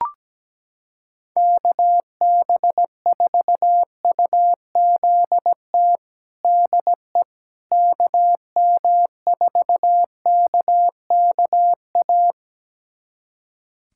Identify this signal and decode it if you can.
Morse code, heard as a keyed tone, about 17 wpm, 700 Hz. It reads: KB4UZT DE KM4KKA
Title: USA Callsign DE Callsign - 6 Characters - 17wpm